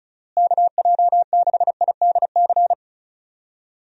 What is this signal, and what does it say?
Morse code, heard as a keyed tone, about 36 wpm, 700 Hz. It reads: KJ6IDC
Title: USA Call Signs - 6 Characters - Repeated 2x - 35wpm